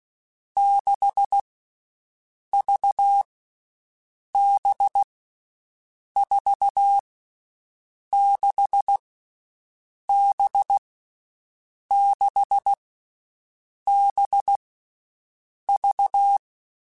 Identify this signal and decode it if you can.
Morse code, heard as a keyed tone, about 16 wpm, 800 Hz. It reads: 6 V B 4 6 B 6 B V
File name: Groups@10